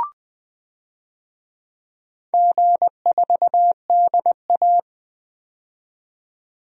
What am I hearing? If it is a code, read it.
Morse code, heard as a keyed tone, about 20 wpm, 700 Hz. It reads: G4DA
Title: DXCC-100 Call Signs - 4 Characters - 20wpm